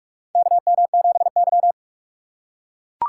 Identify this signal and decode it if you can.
Morse code, heard as a keyed tone, about 43 wpm, 700 Hz. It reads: KM7Y
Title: USA Call Signs - 4 Characters - 8x Character Spacing - 45wpm